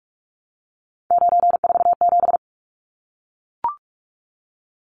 Morse code, heard as a keyed tone, about 44 wpm, 700 Hz. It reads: 947
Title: Numbers - 3-Digit 45wpm